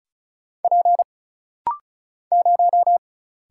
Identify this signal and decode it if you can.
Morse code, heard as a keyed tone, about 34 wpm, 700 Hz. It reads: P 0
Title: Single Letter-Number Alt - Rapid-Fire 35wpm